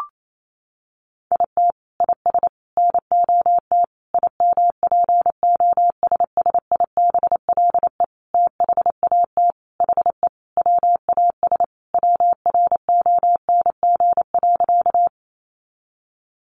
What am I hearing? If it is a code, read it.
Morse code, heard as a keyed tone, about 28 wpm, 700 Hz. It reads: IT IS NOT IMPOSSIBLE THAT HE WAS WRONG.
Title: Sentences from Top 1100 Words 28wpm